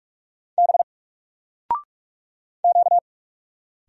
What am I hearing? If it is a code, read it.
Morse code, heard as a keyed tone, about 44 wpm, 700 Hz. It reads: B Q
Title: Single Letter-Number Alt 45wpm